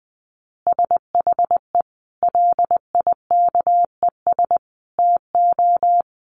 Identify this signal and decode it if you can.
Morse code, heard as a keyed tone, about 20 wpm, 700 Hz. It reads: SHE LIKES TO
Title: Additional Easy Phrases 20wpm